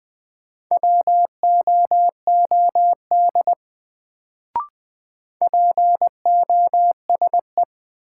Morse code, heard as a keyed tone, about 20 wpm, 700 Hz. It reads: WOOD POSE
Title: Sentences from Top 1000 Words - Review Part 2 - New Word Drill - 20wpm